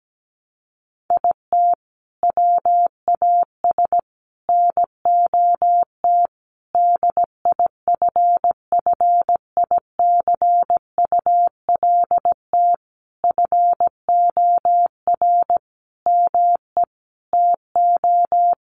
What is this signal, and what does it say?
Morse code, heard as a keyed tone, about 17 wpm, 700 Hz. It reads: IT WAS NOT DIFFICULT FOR ME TO
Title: Sentences from Top 1200 Words 17wpm